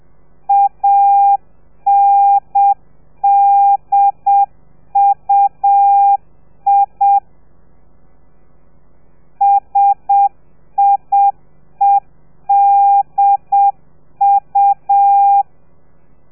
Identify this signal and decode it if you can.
Morse code, heard as a keyed tone, about 7 wpm, 800 Hz. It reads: ANDUI SIEDU
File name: telegrafia / i2tfj / Lez3 30